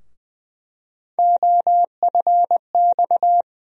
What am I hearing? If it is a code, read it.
Morse code, heard as a keyed tone, about 20 wpm, 700 Hz. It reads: OFX
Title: USA Call Sign Postfixes - 3 Letters 20wpm